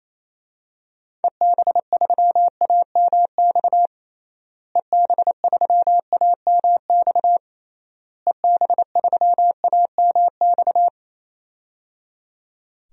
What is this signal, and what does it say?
Morse code, heard as a keyed tone, about 28 wpm, 700 Hz. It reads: EB3AMX EB3AMX EB3AMX
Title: DXCC-100 Call Signs - 6 Characters - Repeated 3x - 28wpm